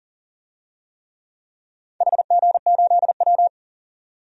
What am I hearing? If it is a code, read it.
Morse code, heard as a keyed tone, about 41 wpm, 700 Hz. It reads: HG8W
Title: DXCC-100 Call Signs - 4 Characters - 40wpm